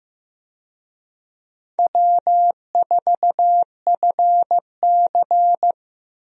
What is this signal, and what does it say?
Morse code, heard as a keyed tone, about 15 wpm, 700 Hz. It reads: W4FC
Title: USA Call Signs - 4 Characters - Repeated 2x - 15wpm